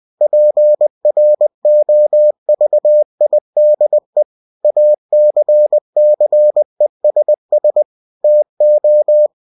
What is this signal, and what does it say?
Morse code, heard as a keyed tone, about 20 wpm, 600 Hz. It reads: PROVIDE ACCESS TO
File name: cw-books / winnie-the-pooh / winnie-20wpm-page-0116